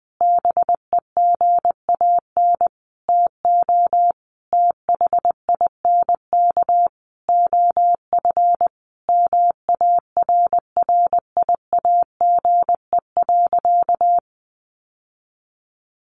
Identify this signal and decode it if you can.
Morse code, heard as a keyed tone, about 19 wpm, 700 Hz. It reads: BEGAN TO THINK OF MARRIAGE.
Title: Sentences from Top 800 Words 20wpm